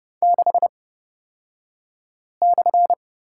Morse code, heard as a keyed tone, about 30 wpm, spaced as for 5 wpm, 700 Hz. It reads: 6/
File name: Lesson 38d - Letter q - Two Characters 30wpm